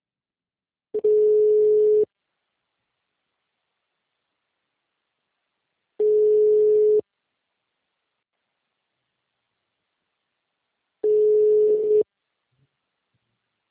Ringback tone (Europe, 425 Hz).
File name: out- -2004- - - .27